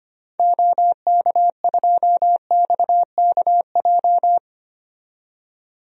Morse code, heard as a keyed tone, about 25 wpm, 700 Hz. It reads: OK2XKJ
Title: DXCC-100 Call Signs - 6 Characters - 25wpm